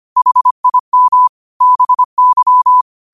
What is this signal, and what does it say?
Morse code, heard as a keyed tone, about 25 wpm, 1000 Hz. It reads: HIM BY
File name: qotdfiles / 25wpm